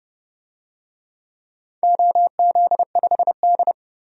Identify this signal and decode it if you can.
Morse code, heard as a keyed tone, about 30 wpm, 700 Hz. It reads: OZ5D
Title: DXCC-100 Call Signs - 4 Characters - 30wpm